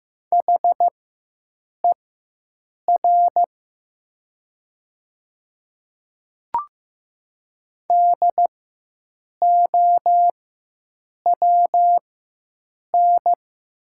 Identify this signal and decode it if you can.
Morse code, heard as a keyed tone, about 15 wpm, spaced as for 7 wpm, 700 Hz. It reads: HER DOWN
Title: 4x Character Spacing - Top 200 Words - 15wpm